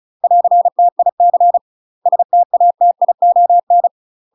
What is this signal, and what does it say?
Morse code, heard as a keyed tone, about 36 wpm, 700 Hz. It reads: <AR>TIC STATION
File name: UK MASTER QSOs TRAINING / 35WPM / UK MASTER QSOs 0023